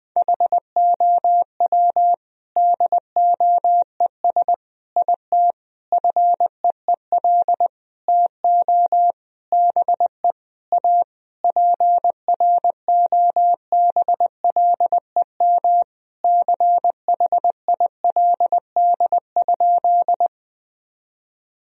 Morse code, heard as a keyed tone, about 20 wpm, 700 Hz. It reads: HOW DOES IT FEEL TO BE A PROBLEM CHILD?
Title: Sentences from Top 1000 Words - Review Part 2 - 20wpm